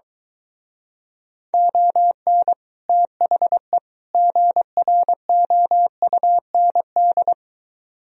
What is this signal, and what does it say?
Morse code, heard as a keyed tone, about 23 wpm, 700 Hz. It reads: ON THE GROUND